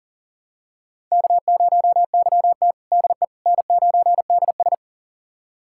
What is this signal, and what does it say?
Morse code, heard as a keyed tone, about 40 wpm, 700 Hz. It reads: K0YT DE N9DS
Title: USA Callsign DE Callsign - 4 Characters - 40wpm